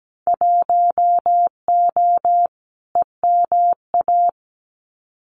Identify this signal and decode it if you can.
Morse code, heard as a keyed tone, about 17 wpm, 700 Hz. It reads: 1O EMA
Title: Winter Field Day Exchanges 17wpm